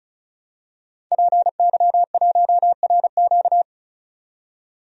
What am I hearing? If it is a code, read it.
Morse code, heard as a keyed tone, about 36 wpm, 700 Hz. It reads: PY1RQ
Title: DXCC-100 Call Signs - 5 Characters - 35wpm